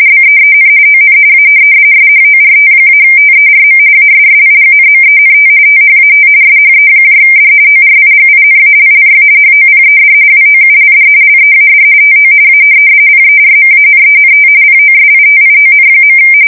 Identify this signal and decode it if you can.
RTTY, about 45.45 baud. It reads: BEEN ON A SHORT-TERM ASSIGNMENT THAT SHED ORIGINALLY HOPED WOULD LEAD TO A PERMANENT POSITION.
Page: 170 Shift 2125/2295